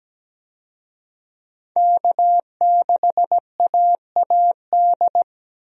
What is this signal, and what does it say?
Morse code, heard as a keyed tone, about 17 wpm, 700 Hz. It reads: K6AAD
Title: USA Call Signs - 5 Characters - 17wpm